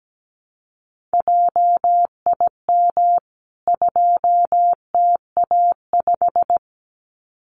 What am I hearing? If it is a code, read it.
Morse code, heard as a keyed tone, about 17 wpm, 700 Hz. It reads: JIM 2TA5
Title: CWT Exchange 17wpm